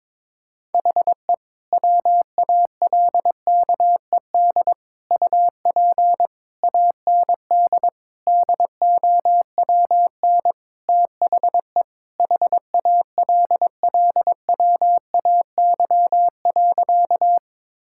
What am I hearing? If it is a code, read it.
Morse code, heard as a keyed tone, about 22 wpm, 700 Hz. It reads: HE WALKED UP AND DOWN THE HALLWAY.
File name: practice / 22wpm / Sentences with 4-Word Phrases - 22wpm